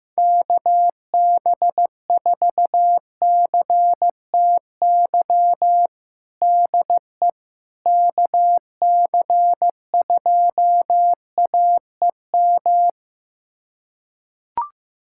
Morse code, heard as a keyed tone, about 15 wpm, 700 Hz. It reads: KB4CTY DE KC2AEM
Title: USA Callsign DE Callsign 15wpm